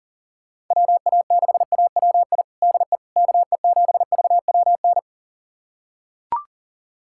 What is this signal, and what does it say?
Morse code, heard as a keyed tone, about 40 wpm, 700 Hz. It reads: WA6AWI DE KE7VWN
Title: USA Callsign DE Callsign - 6 Characters - 40wpm